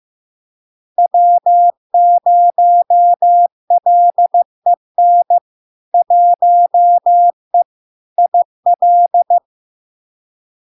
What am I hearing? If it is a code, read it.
Morse code, heard as a keyed tone, about 15 wpm, 700 Hz. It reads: W0LEN 1E IL
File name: LARK-FD-2022-Log-in-CW-15wpm-700-Hz